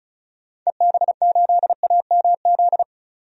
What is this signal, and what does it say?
Morse code, heard as a keyed tone, about 34 wpm, 700 Hz. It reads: EB8AMZ
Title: DXCC-100 Call Signs - 6 Characters - 35wpm